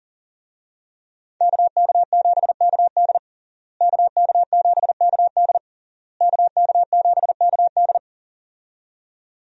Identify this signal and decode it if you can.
Morse code, heard as a keyed tone, about 40 wpm, 700 Hz. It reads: KK7KD KK7KD KK7KD
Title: USA Call Signs - 5 Characters - Repeated 3x - 40wpm